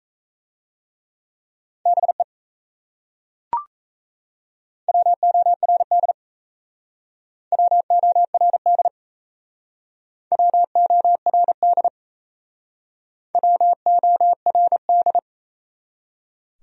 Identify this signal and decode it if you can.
Morse code, heard as a keyed tone, about 38 wpm, 700 Hz. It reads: BE WORD WORD WORD WORD